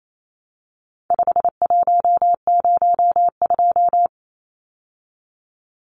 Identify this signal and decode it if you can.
Morse code, heard as a keyed tone, about 28 wpm, 700 Hz. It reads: 5102
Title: Numbers - 4-Digit 28wpm